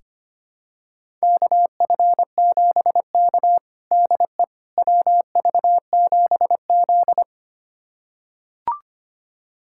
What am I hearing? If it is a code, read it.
Morse code, heard as a keyed tone, about 25 wpm, 700 Hz. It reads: KF7K DE WV7Z
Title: USA Callsign DE Callsign - 4 Characters - 25wpm